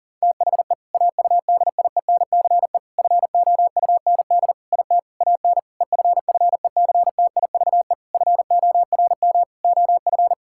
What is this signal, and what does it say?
Morse code, heard as a keyed tone, about 40 wpm, 690 Hz. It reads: THE AUDIENCE FOUND IT AN EFFECTIVE FORM OF
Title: Sentences from Top 1000 Words - Review Part 6 - 40wpm